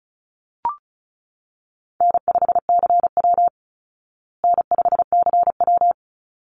Morse code, heard as a keyed tone, about 34 wpm, 700 Hz. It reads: N5CW N5CW
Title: Easy Call Signs Repeated 2x - 35wpm